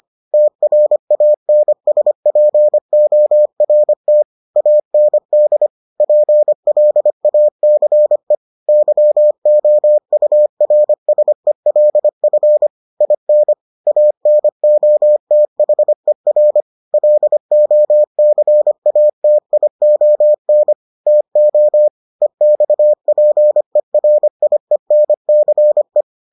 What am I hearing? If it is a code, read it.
Morse code, heard as a keyed tone, about 25 wpm, 600 Hz. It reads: TRANSPORT AND PLACE YOURSELF IN ANOTHER LOCATION TO EXPERIENCE